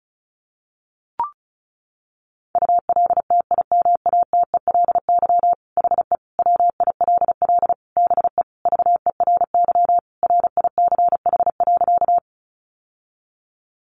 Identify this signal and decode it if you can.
Morse code, heard as a keyed tone, about 34 wpm, 700 Hz. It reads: ULTIMATELY HE WILL BE VERY RICH.
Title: Sentences from Top 1200 Words 35wpm